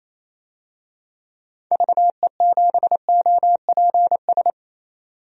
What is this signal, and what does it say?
Morse code, heard as a keyed tone, about 28 wpm, 700 Hz. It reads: VE7OPS
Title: DXCC-100 Call Signs - 6 Characters - 28wpm